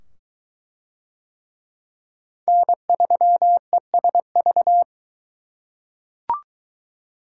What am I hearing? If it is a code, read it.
Morse code, heard as a keyed tone, about 20 wpm, 700 Hz. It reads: N3ESV